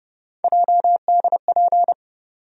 Morse code, heard as a keyed tone, about 30 wpm, 700 Hz. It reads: JDP
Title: USA Call Sign Postfixes - 3 Letters 30wpm